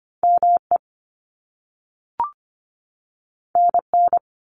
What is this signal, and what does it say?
Morse code, heard as a keyed tone, about 25 wpm, 700 Hz. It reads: ME NN
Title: Top Two Letter Combinations in Words 25wpm